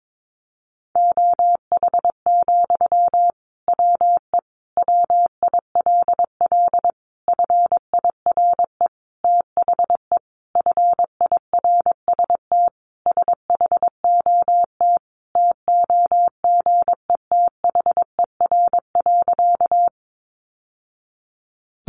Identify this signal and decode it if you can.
Morse code, heard as a keyed tone, about 22 wpm, 700 Hz. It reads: OH, WE WILL FIRE THE FIRST SHOT TOGETHER.
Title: Sentences from Top 1100 Words 22wpm